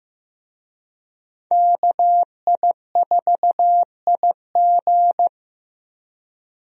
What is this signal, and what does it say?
Morse code, heard as a keyed tone, about 15 wpm, 700 Hz. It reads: KI4IG
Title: USA Call Signs - 5 Characters - Repeated 3x - 15wpm